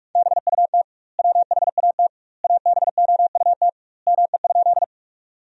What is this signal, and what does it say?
Morse code, heard as a keyed tone, about 43 wpm, 700 Hz. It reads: BUT WHAT ABOUT ME?
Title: 2x Word-Spacing - Sentences from Top 200 Words - 45wpm